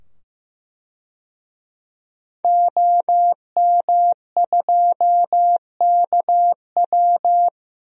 Morse code, heard as a keyed tone, about 15 wpm, 700 Hz. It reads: OM2KW
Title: DXCC-100 Call Signs - 5 Characters - 15wpm